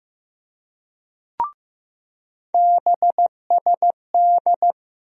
Morse code, heard as a keyed tone, about 15 wpm, 700 Hz. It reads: BSD
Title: USA Call Sign Postfixes - 3 Letters 15wpm